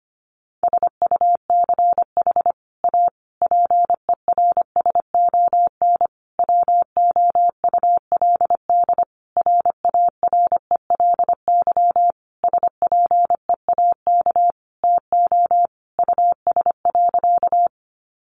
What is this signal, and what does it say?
Morse code, heard as a keyed tone, about 25 wpm, 700 Hz. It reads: SUCH A PERSON WOULD RARELY SPEAK TO US.
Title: Sentences from Top 1400 Words 25wpm